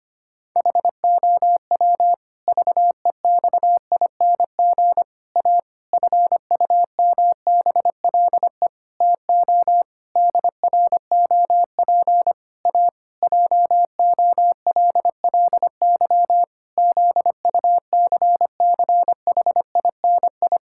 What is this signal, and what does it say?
Morse code, heard as a keyed tone, about 25 wpm, 700 Hz. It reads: HOW VEXING A FUMBLE TO DROP A JOLLY ZUCCHINI
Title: Pangram Challenge v2 - 25wpm